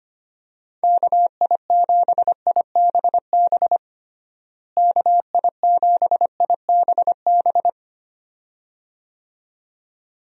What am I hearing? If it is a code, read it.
Morse code, heard as a keyed tone, about 25 wpm, 700 Hz. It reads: KI7IBB KI7IBB
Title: USA Call Signs - 6 Characters - Repeated 2x - 25wpm